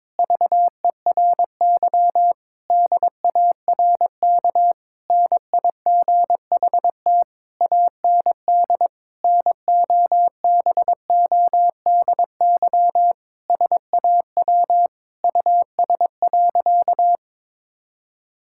Morse code, heard as a keyed tone, about 22 wpm, 700 Hz. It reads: VERY DARK NIGHT AND NOBODY SAW US.
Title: Sentences from Top 1600 Words 22wpm